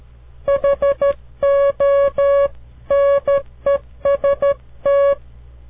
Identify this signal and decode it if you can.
Morse code, heard as a keyed tone, about 13 wpm, 560 Hz. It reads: HONEST
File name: CW Learning Programs / farnsworth / unit 2